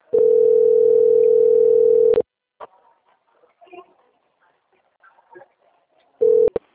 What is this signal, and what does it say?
Ringback tone (North America), 440 + 480 Hz.
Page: Campo Grande - RJ